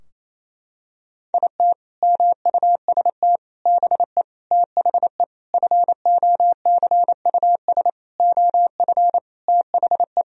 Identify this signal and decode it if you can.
Morse code, heard as a keyed tone, about 28 wpm, 700 Hz. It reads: IT MUST BE THE FOCUS OF THE
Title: Sentences from Top 1000 Words - Review Part 4 - 28wpm